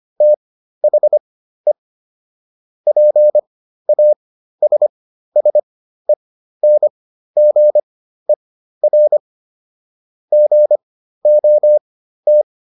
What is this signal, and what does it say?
Morse code, heard as a keyed tone, about 25 wpm, spaced as for 13 wpm, 600 Hz. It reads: THE PASSENGER GOT